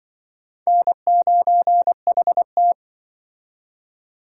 Morse code, heard as a keyed tone, about 25 wpm, 700 Hz. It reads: N9HT